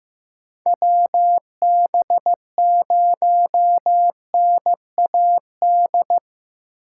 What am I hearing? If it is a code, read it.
Morse code, heard as a keyed tone, about 15 wpm, 700 Hz. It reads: WB0NAD
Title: USA Call Signs - 6 Characters - 6x Character Spacing - 15wpm